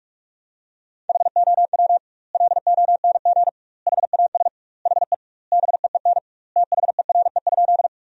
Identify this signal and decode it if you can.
Morse code, heard as a keyed tone, about 46 wpm, 700 Hz. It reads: HOW LONG HAS HE BEEN THERE?
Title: 2x Word-Spacing - Sentences from Top 100 Words - 45wpm